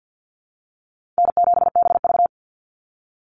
Morse code, heard as a keyed tone, about 50 wpm, 700 Hz. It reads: N7BV
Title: Easy Call Signs Repeated 3x - 50wpm